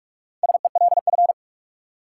Morse code, heard as a keyed tone, about 46 wpm, 700 Hz. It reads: SELF
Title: Sentences from Top 1000 Words - Review Part 1 - New Word Drill - 45wpm